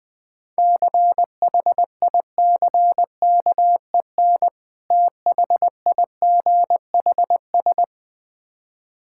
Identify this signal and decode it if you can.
Morse code, heard as a keyed tone, about 20 wpm, 690 Hz. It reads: CHICKEN THIGHS
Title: Difficult Words with Hints - 20wpm